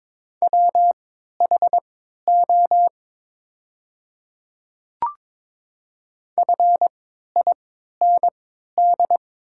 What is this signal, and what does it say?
Morse code, heard as a keyed tone, about 22 wpm, spaced as for 13 wpm, 690 Hz. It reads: WHO FIND
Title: 3x Character Spacing - Top 100 Words - 22wpm